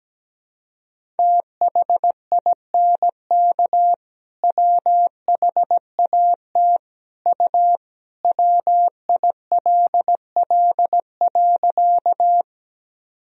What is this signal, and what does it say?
Morse code, heard as a keyed tone, about 17 wpm, 700 Hz. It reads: THINK WHAT U WILL.
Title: Sentences from Top 200 Words and Abbreviations 17wpm